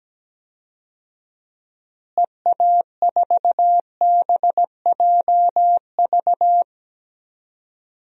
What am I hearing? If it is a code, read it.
Morse code, heard as a keyed tone, about 17 wpm, 700 Hz. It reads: EA4BJV
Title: DXCC-100 Call Signs - 6 Characters - Repeated 3x - 17wpm